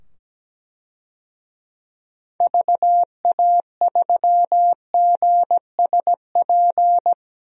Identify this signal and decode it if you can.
Morse code, heard as a keyed tone, about 17 wpm, 700 Hz. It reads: VA3GSP
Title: DXCC-100 Call Signs - 6 Characters - 17wpm